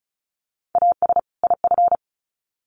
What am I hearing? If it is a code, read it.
Morse code, heard as a keyed tone, about 34 wpm, 700 Hz. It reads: AS IF
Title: Sets of 2 Words 35wpm